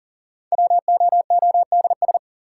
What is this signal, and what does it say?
Morse code, heard as a keyed tone, about 40 wpm, 700 Hz. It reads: WOODS
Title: Sentences from Top 1000 Words - Review Part 2 - New Word Drill - 40wpm